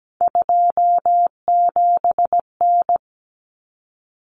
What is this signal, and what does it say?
Morse code, heard as a keyed tone, about 17 wpm, 700 Hz. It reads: 27N
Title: Common Cut Numbers - 3-Digit 17wpm